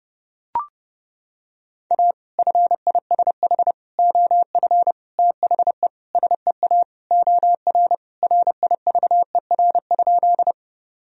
Morse code, heard as a keyed tone, about 30 wpm, 700 Hz. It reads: A FISH OF THE SEA OR RIVER?
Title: Sentences from Top 1000 Words - Review Part 1 - 30wpm